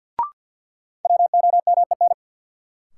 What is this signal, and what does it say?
Morse code, heard as a keyed tone, about 53 wpm, 700 Hz. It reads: WOMEN
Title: Sentences from Top 1300 Words - New Word Drill Alt - 50wpm